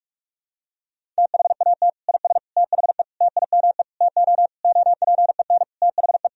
Morse code, heard as a keyed tone, about 46 wpm, 700 Hz. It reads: THAT IS THE TIME TO OPEN THE
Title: Sentences from Top 1000 Words - Review Part 1 - 45wpm